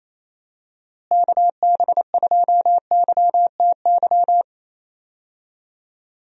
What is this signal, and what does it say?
Morse code, heard as a keyed tone, about 27 wpm, 700 Hz. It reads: KB2YTY
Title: USA Call Signs - 6 Characters - Repeated 3x - 28wpm